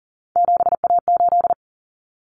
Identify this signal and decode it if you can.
Morse code, heard as a keyed tone, about 40 wpm, 700 Hz. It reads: 7A8
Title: Common Cut Numbers - 3-Digit 40wpm